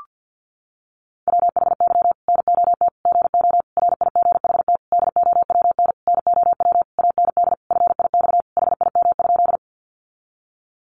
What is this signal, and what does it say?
Morse code, heard as a keyed tone, about 50 wpm, 700 Hz. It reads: WHY NOT GO RIGHT DOWN NOW AND FIX HIM?
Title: Sentences from Top 1400 Words 50wpm